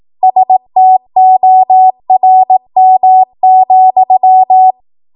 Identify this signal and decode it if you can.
Morse code, heard as a keyed tone, about 18 wpm, 750 Hz. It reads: STORM,
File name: Archive / 18 WPM / 18WPM